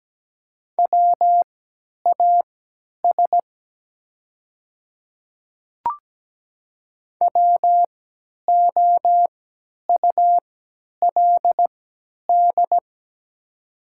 Morse code, heard as a keyed tone, about 17 wpm, spaced as for 10 wpm, 700 Hz. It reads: WAS WOULD
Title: 3x Character Spacing - Top 100 Words - 17wpm